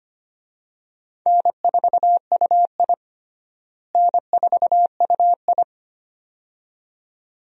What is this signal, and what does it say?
Morse code, heard as a keyed tone, about 25 wpm, 700 Hz. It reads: N4UI N4UI
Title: USA Call Signs - 4 Characters - Repeated 2x - 25wpm